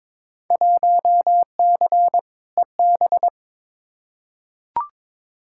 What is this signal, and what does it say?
Morse code, heard as a keyed tone, about 22 wpm, 700 Hz. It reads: 1C EB
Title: Field Day Exchanges Encore - 22wpm